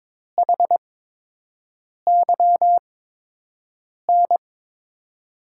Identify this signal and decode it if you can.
Morse code, heard as a keyed tone, about 22 wpm, spaced as for 6 wpm, 700 Hz. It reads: HYN
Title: USA Call Signs - 6 Characters - 8x Character Spacing - 22wpm